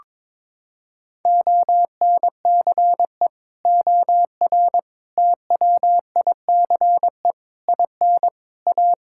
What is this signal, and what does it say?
Morse code, heard as a keyed tone, about 22 wpm, 700 Hz. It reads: ONCE OR TWICE IN A
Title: Sentences from Top 1600 Words 22wpm